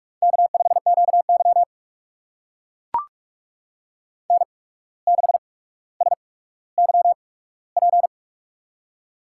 Morse code, heard as a keyed tone, about 44 wpm, spaced as for 18 wpm, 700 Hz. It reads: K5QY N6SYP
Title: USA Call Signs - 8x Character Spacing - 45wpm